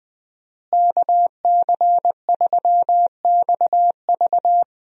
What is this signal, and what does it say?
Morse code, heard as a keyed tone, about 20 wpm, 700 Hz. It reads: KC3XV
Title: USA Call Signs - 5 Characters - Repeated 2x - 20wpm